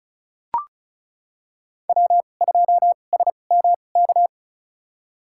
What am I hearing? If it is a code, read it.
Morse code, heard as a keyed tone, about 36 wpm, spaced as for 25 wpm, 700 Hz. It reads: W2SMK
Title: USA Call Signs - 5 Characters - 2x Character Spacing - 35wpm